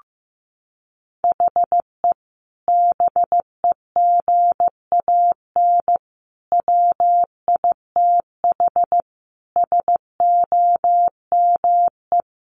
Morse code, heard as a keyed tone, about 15 wpm, 700 Hz. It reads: HE BEGAN WITH SOME
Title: Sentences from Top 1500 Words 15wpm